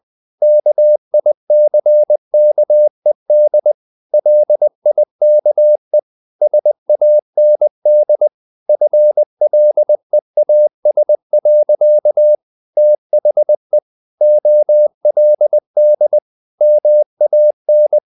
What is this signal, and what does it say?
Morse code, heard as a keyed tone, about 20 wpm, 600 Hz. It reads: KICKED LIKE SAND FLEAS. THE OLD MAN
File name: the old man and the sea part 2 20wpm